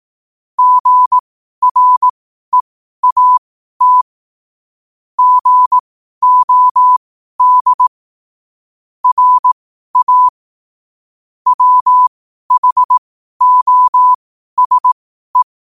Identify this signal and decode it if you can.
Morse code, heard as a keyed tone, about 18 wpm, spaced as for 12 wpm, 1000 Hz. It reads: GREAT GOD RA WHOSE
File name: qotdfiles / 12wpm